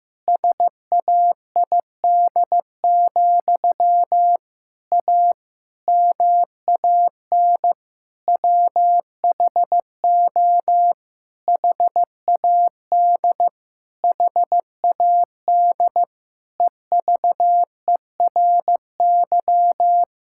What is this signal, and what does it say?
Morse code, heard as a keyed tone, about 16 wpm, 700 Hz. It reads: SAID, A MAN WHO HAD HAD EVERY
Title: Sentences from Top 1000 Words - Review Part 6 - 15wpm